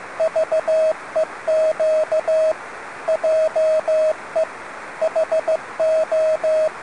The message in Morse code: VEQ JE HO